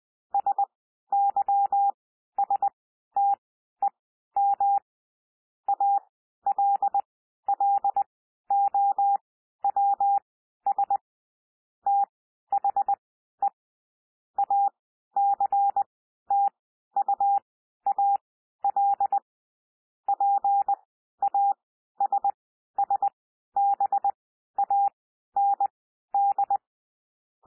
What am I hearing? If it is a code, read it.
Morse code, heard as a keyed tone, about 20 wpm, spaced as for 12 wpm, 800 Hz. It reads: SYSTEM ALLOWS THE ACTUAL PASSBAND
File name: cwmp3 / P1 SOUND PASSAGE 20 wpm